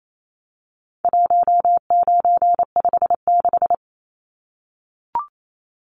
Morse code, heard as a keyed tone, about 28 wpm, 700 Hz. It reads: 1956
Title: Numbers - 4-Digit 28wpm